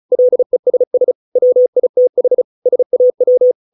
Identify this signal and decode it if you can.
Morse code, heard as a keyed tone, about 34 wpm, 500 Hz. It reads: LESS WITH SAW